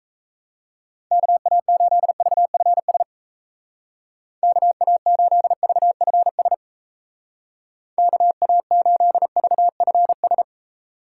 Morse code, heard as a keyed tone, about 37 wpm, 700 Hz. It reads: KA8VFS KA8VFS KA8VFS